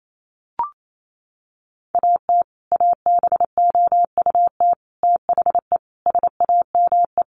Morse code, heard as a keyed tone, about 28 wpm, 700 Hz. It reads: AT ABOUT THE SAME
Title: Sets of 5 Words Encore 28wpm